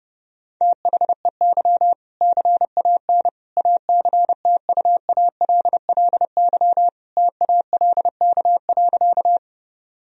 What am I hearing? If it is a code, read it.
Morse code, heard as a keyed tone, about 30 wpm, 700 Hz. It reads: THEY CAN ACTUALLY TALK.
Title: Sentences from Top 1000 Words - Review Part 4 - 30wpm